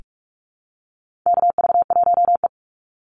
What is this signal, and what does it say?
Morse code, heard as a keyed tone, about 44 wpm, 700 Hz. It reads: KV1E
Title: Call Signs - 45wpm